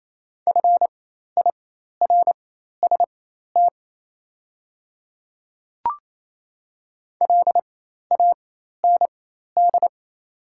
Morse code, heard as a keyed tone, about 28 wpm, spaced as for 13 wpm, 700 Hz. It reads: FIRST LAND
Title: 4x Character Spacing - Top 300 Words - 28wpm